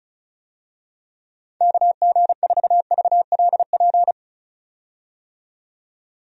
Morse code, heard as a keyed tone, about 36 wpm, 700 Hz. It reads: KG4VLP
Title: USA Call Signs - 6 Characters - 35wpm